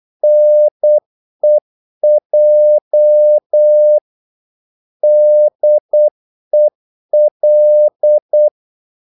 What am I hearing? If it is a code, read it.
Morse code, heard as a keyed tone, about 8 wpm, 600 Hz. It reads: NEJ DEL
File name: Korte ord DK 8wpm